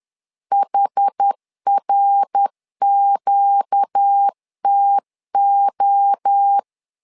Morse code, heard as a keyed tone, about 10 wpm, 800 Hz. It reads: HRQTO
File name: LEZIONI / 4 Q 10wpm 2210